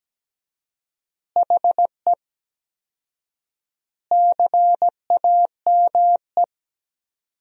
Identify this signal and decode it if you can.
Morse code, heard as a keyed tone, about 17 wpm, 700 Hz. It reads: HE CAME
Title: 4x Word-Spacing - Sentences from Top 500 Words - 17wpm